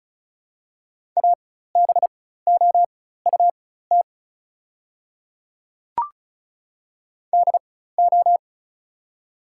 Morse code, heard as a keyed tone, about 36 wpm, spaced as for 16 wpm, 700 Hz. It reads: ABOUT DO
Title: 4x Character Spacing - Top 100 Words - 35wpm